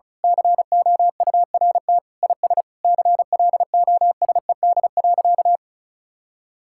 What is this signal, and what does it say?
Morse code, heard as a keyed tone, about 36 wpm, 700 Hz. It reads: COURT IS CLOSED.
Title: Sentences from Top 1600 Words 35wpm